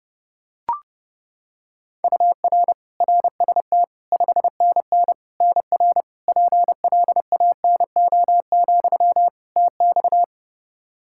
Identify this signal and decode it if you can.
Morse code, heard as a keyed tone, about 32 wpm, 700 Hz. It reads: UR RST 5NN NR PLANO, TX
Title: Practice QSO - Exchange 1 - Beginning - 30wpm